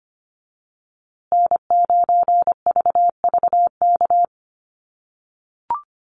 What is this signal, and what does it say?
Morse code, heard as a keyed tone, about 25 wpm, 700 Hz. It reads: N9VVK
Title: USA Call Signs Repeated 2x 25wpm